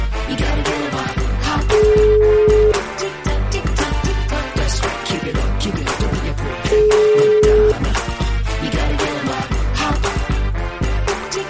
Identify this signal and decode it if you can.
Ringback tone (Europe, 425 Hz), with noise.